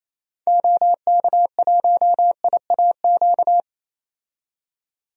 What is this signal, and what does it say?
Morse code, heard as a keyed tone, about 27 wpm, 700 Hz. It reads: OK1IAQ
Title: DXCC-100 Call Signs - 6 Characters - 28wpm